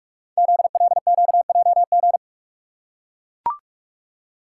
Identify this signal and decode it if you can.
Morse code, heard as a keyed tone, about 44 wpm, 690 Hz. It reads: ZLQJG
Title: Code Groups - 5 Letters - 45wpm